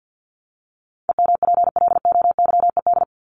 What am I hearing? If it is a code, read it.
Morse code, heard as a keyed tone, about 50 wpm, 700 Hz. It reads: EMPLOYED
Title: ED Suffix - Top 100 Words with Base Words 50wpm